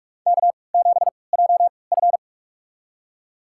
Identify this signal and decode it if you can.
Morse code, heard as a keyed tone, about 47 wpm, spaced as for 26 wpm, 700 Hz. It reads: K7JF
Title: USA Call Signs - 4 Characters - 3x Character Spacing - 45wpm